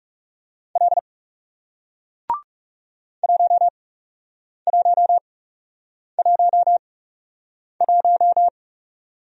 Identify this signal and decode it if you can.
Morse code, heard as a keyed tone, about 39 wpm, spaced as for 8 wpm, 700 Hz. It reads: L 1111